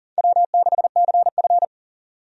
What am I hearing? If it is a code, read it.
Morse code, heard as a keyed tone, about 40 wpm, 700 Hz. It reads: W6CF
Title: USA Call Signs - 4 Characters - 40wpm